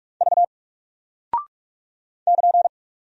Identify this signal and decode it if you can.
Morse code, heard as a keyed tone, about 46 wpm, 700 Hz. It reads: V <KN>
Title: Single Character (Letter:Number:Punctuation:Prosign) Alt 45wpm